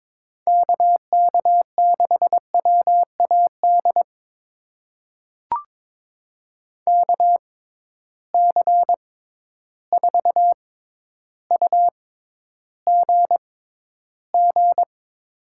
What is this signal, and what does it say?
Morse code, heard as a keyed tone, about 22 wpm, spaced as for 11 wpm, 700 Hz. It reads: KK6WAD KC4UGG
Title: USA Call Signs - 6 Characters - 6x Character Spacing - 22wpm